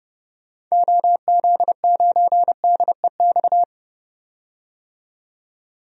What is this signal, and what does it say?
Morse code, heard as a keyed tone, about 30 wpm, 700 Hz. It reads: OZ9DEX
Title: DXCC-100 Call Signs - 6 Characters - 30wpm